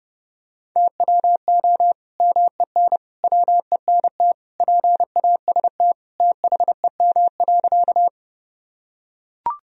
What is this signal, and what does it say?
Morse code, heard as a keyed tone, about 30 wpm, 700 Hz. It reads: TWO MEN WENT PAST THEM.
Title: Sentences from Top 1000 Words - Review Part 3 - 30wpm